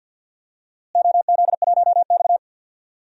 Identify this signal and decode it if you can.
Morse code, heard as a keyed tone, about 52 wpm, 700 Hz. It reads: OZ1X
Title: DXCC-100 Call Signs - 4 Characters - Repeated 3x - 50wpm